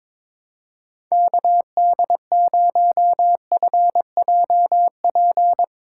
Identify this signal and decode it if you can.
Morse code, heard as a keyed tone, about 22 wpm, 700 Hz. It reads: KD0FJP
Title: USA Call Signs - 6 Characters - 8x Character Spacing - 22wpm